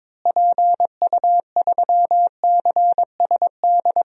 Morse code, heard as a keyed tone, about 22 wpm, 700 Hz. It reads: PU3CSD
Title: DXCC-100 Call Signs - 6 Characters - Repeated 2x - 22wpm